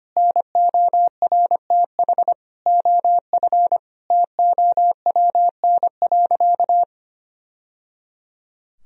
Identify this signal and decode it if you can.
Morse code, heard as a keyed tone, about 25 wpm, 700 Hz. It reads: NORTH OF TOWN.
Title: Sentences from Top 1000 Words - Review Part 4 - 25wpm